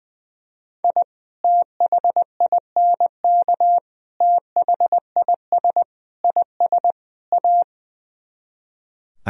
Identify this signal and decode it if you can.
Morse code, heard as a keyed tone, about 20 wpm, 700 Hz. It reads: I THINK THIS IS A
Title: Speed-Racing - Sets of 5 Words 20wpm